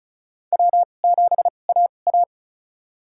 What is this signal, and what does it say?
Morse code, heard as a keyed tone, about 36 wpm, spaced as for 25 wpm, 700 Hz. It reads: W7AA
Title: USA Call Signs - 4 Characters - 2x Character Spacing - 35wpm